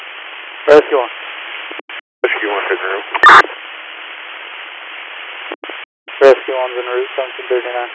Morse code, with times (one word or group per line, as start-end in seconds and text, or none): none